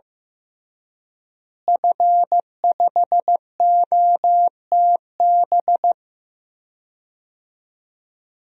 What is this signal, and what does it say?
Morse code, heard as a keyed tone, about 15 wpm, 700 Hz. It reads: F5OTB
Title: DXCC-100 Call Signs - 5 Characters - Repeated 3x - 15wpm